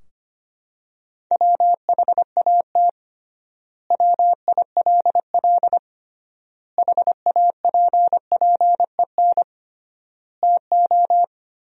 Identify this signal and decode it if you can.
Morse code, heard as a keyed tone, about 25 wpm, 700 Hz. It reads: WHAT WILL HAPPEN TO
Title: 3x Word-Spacing - Sentences from Top 400 Words - 25wpm